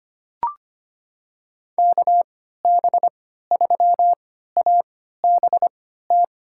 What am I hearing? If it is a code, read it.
Morse code, heard as a keyed tone, about 25 wpm, spaced as for 14 wpm, 700 Hz. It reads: KB3ABT
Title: USA Call Signs - 6 Characters - 3x Character Spacing - 25wpm